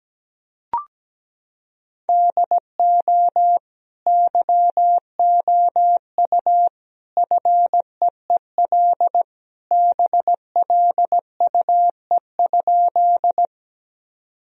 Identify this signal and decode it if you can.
Morse code, heard as a keyed tone, about 17 wpm, 700 Hz. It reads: DO YOU FEEL BLUE?
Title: Sentences from Top 1000 Words - Review Part 3 - 17wpm